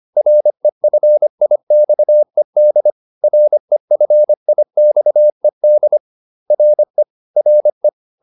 Morse code, heard as a keyed tone, about 25 wpm, 600 Hz. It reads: REFIXED REFIXED RE RE